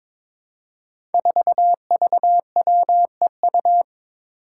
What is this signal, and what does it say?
Morse code, heard as a keyed tone, about 22 wpm, 700 Hz. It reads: 4VWEU
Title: Five Challenging Letter-Number Combinations 22wpm